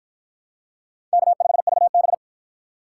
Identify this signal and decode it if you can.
Morse code, heard as a keyed tone, about 46 wpm, 700 Hz. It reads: K5VB